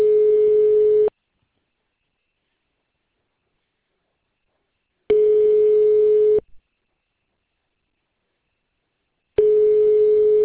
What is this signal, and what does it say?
Ringback tone (425 Hz, 1.5 s on, 3 s off).